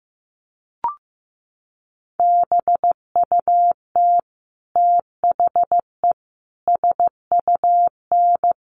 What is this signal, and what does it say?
Morse code, heard as a keyed tone, about 15 wpm, 700 Hz. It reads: BUT THE SUN
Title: Sentences from Top 1100 Words 15wpm